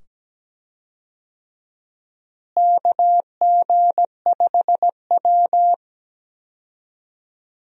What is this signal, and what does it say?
Morse code, heard as a keyed tone, about 17 wpm, 700 Hz. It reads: KG5W
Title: USA Call Signs - 4 Characters - Repeated 2x - 17wpm